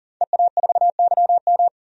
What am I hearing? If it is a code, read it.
Morse code, heard as a keyed tone, about 40 wpm, 700 Hz. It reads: EA4YM
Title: DXCC-100 Call Signs - 5 Characters - 40wpm